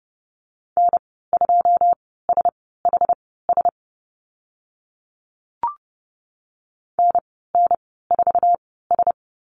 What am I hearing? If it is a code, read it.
Morse code, heard as a keyed tone, about 30 wpm, spaced as for 17 wpm, 700 Hz. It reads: N2SHS NN4S
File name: Lesson 14g - Number 2 - Call Signs 30wpm